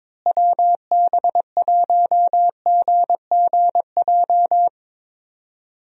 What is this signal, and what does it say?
Morse code, heard as a keyed tone, about 22 wpm, 700 Hz. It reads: WB1GGJ
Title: USA Call Signs - 6 Characters - Repeated 2x - 22wpm